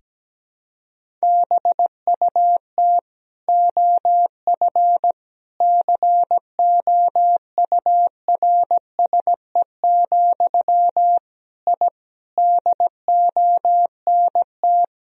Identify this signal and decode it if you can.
Morse code, heard as a keyed tone, about 17 wpm, 700 Hz. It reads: BUT OF COURSE, I DONT
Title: Sentences from Top 1000 Words - Review Part 2 - 17wpm